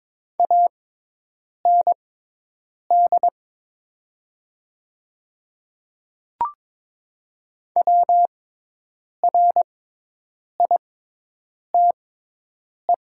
Morse code, heard as a keyed tone, about 22 wpm, spaced as for 8 wpm, 700 Hz. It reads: AND WRITE
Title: 6x Character Spacing - Top 100 Words - 22wpm